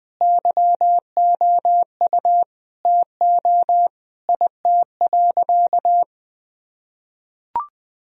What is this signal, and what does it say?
Morse code, heard as a keyed tone, about 20 wpm, 700 Hz. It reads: YOU TO IT.
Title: Sentences from Top 1000 Words - Review Part 2 - 20wpm